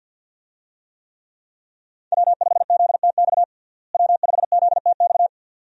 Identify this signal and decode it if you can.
Morse code, heard as a keyed tone, about 53 wpm, 700 Hz. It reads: W5ZTX W5ZTX
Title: USA Call Signs - 5 Characters - Repeated 2x - 50wpm